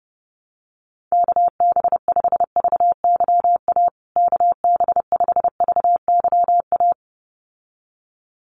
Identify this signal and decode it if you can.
Morse code, heard as a keyed tone, about 30 wpm, 700 Hz. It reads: KB5VYA KB5VYA
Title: USA Call Signs Repeated 2x 30wpm